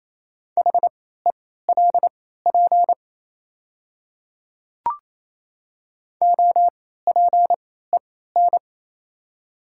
Morse code, heard as a keyed tone, about 28 wpm, spaced as for 16 wpm, 690 Hz. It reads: HELP OPEN
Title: 3x Character Spacing - Top 300 Words - 28wpm